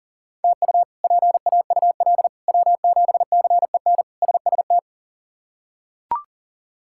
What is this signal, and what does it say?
Morse code, heard as a keyed tone, about 40 wpm, 700 Hz. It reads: TU PAUL W7CEN SST
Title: K1USN SST QSO Practice - 40wpm